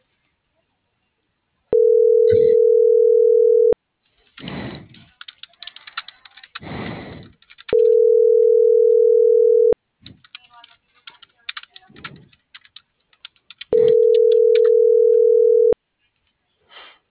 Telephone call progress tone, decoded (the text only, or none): ringback tone